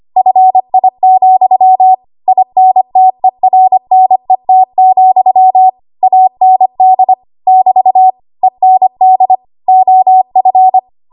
Morse code, heard as a keyed tone, about 25 wpm, 750 Hz. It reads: FI, INTERNET, AND <BT> END OF
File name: Morse / 25WPM